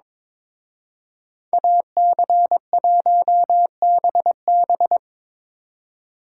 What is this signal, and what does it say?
Morse code, heard as a keyed tone, about 26 wpm, 700 Hz. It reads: AC1BB
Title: Speed Racing - USA 2x2 Call Signs 22wpm